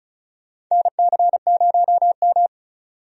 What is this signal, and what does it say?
Morse code, heard as a keyed tone, about 36 wpm, 700 Hz. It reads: NC0M
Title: USA Call Signs - 4 Characters - 35wpm